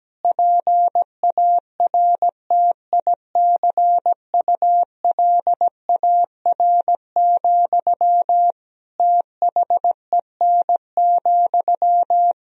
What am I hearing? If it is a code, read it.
Morse code, heard as a keyed tone, about 17 wpm, 700 Hz. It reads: PARTICULAR, THEN,
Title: Sentences from Top 1000 Words - Review Part 4 - 17wpm